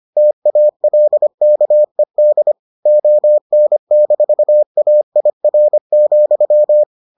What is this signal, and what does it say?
Morse code, heard as a keyed tone, about 25 wpm, 600 Hz. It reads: TALKED ON-AIR,